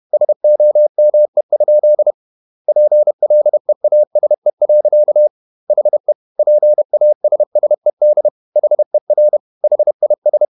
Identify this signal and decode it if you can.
Morse code, heard as a keyed tone, about 31 wpm, 600 Hz. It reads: SOME? PLEASE. HE PASSED HER HIS
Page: Never 31wpm | CW med Gnister